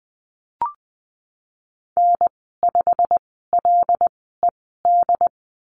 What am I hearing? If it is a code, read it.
Morse code, heard as a keyed tone, about 20 wpm, spaced as for 15 wpm, 700 Hz. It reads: N5LED
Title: 122 Lesson 20g - Number 3 - Call Signs 20wpm